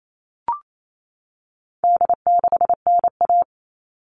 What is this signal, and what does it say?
Morse code, heard as a keyed tone, about 28 wpm, 700 Hz. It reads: D6NA
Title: All Cut Numbers - 4-Digit 28wpm